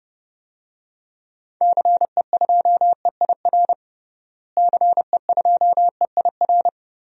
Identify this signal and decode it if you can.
Morse code, heard as a keyed tone, about 31 wpm, 700 Hz. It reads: CE2EIR CE2EIR
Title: DXCC-100 Call Signs - 6 Characters - Repeated 3x - 30wpm